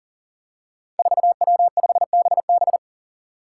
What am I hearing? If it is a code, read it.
Morse code, heard as a keyed tone, about 40 wpm, 700 Hz. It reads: 4W5BB
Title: Five Challenging Letter-Number Combinations 40wpm